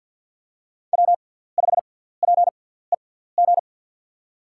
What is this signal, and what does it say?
Morse code, heard as a keyed tone, about 48 wpm, spaced as for 17 wpm, 700 Hz. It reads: W5PEG
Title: USA Call Signs - 5 Characters - 6x Character Spacing - 50wpm